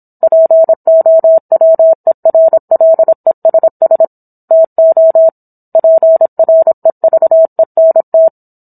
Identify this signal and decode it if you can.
Morse code, heard as a keyed tone, about 26 wpm, 650 Hz. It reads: POWERLESS TO PREVENT